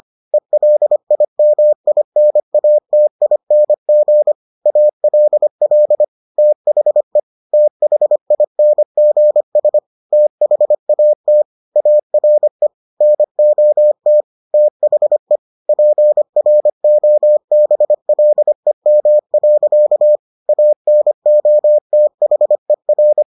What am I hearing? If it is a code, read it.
Morse code, heard as a keyed tone, about 25 wpm, 600 Hz. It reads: ELIMINATING ALL THE THINGS THAT ARE NOT THE PROBLEM. ANOTHER